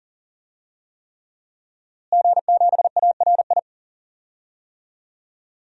Morse code, heard as a keyed tone, about 40 wpm, 700 Hz. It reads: G7ARI
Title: DXCC-100 Call Signs - 5 Characters - Repeated 3x - 40wpm